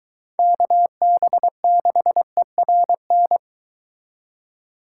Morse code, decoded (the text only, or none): KB6ERN